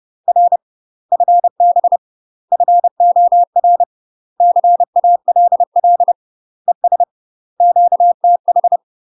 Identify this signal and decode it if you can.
Morse code, heard as a keyed tone, about 30 wpm, 700 Hz. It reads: R FB FOR CALL ES QTH
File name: UK MASTER QSOs TRAINING / 30WPM / UK MASTER QSOs 0023